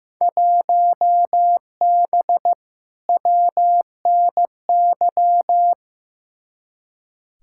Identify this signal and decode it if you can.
Morse code, heard as a keyed tone, about 15 wpm, 700 Hz. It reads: 1B WNY
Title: Field Day Exchanges - 15wpm